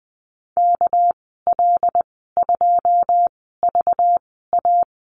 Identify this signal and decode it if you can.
Morse code, heard as a keyed tone, about 20 wpm, spaced as for 15 wpm, 700 Hz. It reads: KL2VA
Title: 207 Lesson 33g - Letter j - Call Signs 20wpm